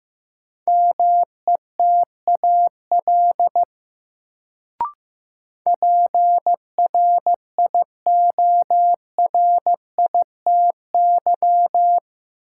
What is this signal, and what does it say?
Morse code, heard as a keyed tone, about 15 wpm, 700 Hz. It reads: METAL PRIORITY
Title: Sentences from Top 1600 Words - New Word Drill Alt - 15wpm